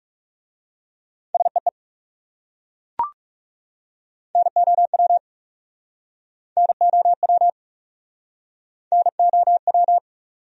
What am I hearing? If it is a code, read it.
Morse code, heard as a keyed tone, about 37 wpm, 700 Hz. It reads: SEE NOW NOW NOW
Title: Speed-Racing Top 500 Words 30wpm